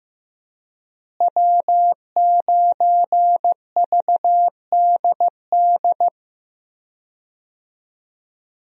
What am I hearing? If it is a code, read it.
Morse code, heard as a keyed tone, about 15 wpm, 700 Hz. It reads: W9VDD
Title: USA Call Signs - 5 Characters - Repeated 2x - 15wpm